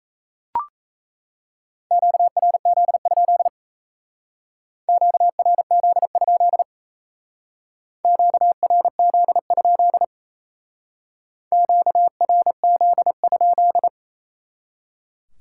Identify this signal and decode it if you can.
Morse code, heard as a keyed tone, about 37 wpm, 700 Hz. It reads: QRZ? QRZ? QRZ? QRZ?